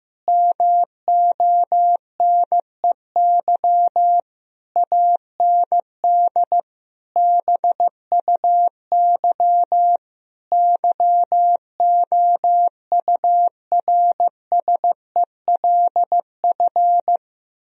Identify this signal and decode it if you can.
Morse code, heard as a keyed tone, about 15 wpm, 700 Hz. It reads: MONEY AND BUY YOURSELF
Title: Sentences from Top 1000 Words - Review Part 4 - 15wpm